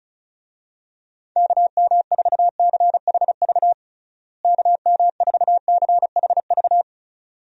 Morse code, heard as a keyed tone, about 36 wpm, 700 Hz. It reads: KM4CHV KM4CHV
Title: USA Call Signs - 6 Characters - Repeated 2x - 35wpm